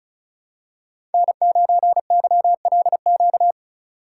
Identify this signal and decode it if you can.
Morse code, heard as a keyed tone, about 36 wpm, 700 Hz. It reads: N9YLQ
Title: USA Call Signs - 3x Character Spacing - 35wpm